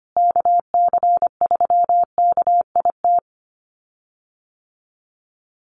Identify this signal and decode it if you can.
Morse code, heard as a keyed tone, about 25 wpm, 700 Hz. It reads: KC3KIT
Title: USA Call Signs Repeated 3x 25wpm